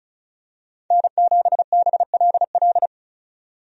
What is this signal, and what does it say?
Morse code, heard as a keyed tone, about 34 wpm, 700 Hz. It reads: N7BLL
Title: USA Call Signs - 5 Characters - 35wpm